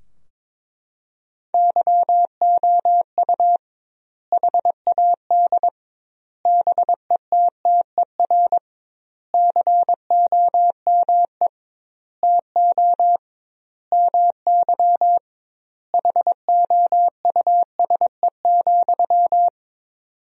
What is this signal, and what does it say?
Morse code, heard as a keyed tone, about 22 wpm, 700 Hz. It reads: YOU HAD BETTER COME TO MY HOUSE,
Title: 2x Word-Spacing - Sentences from Top 400 Words - 22wpm